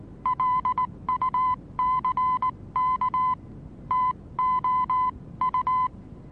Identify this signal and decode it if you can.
Morse code, heard as a keyed tone, about 19 wpm, spaced as for 16 wpm, 1010 Hz. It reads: LUCK TOU
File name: repos / Morsekurs / DJ4UF Morsen26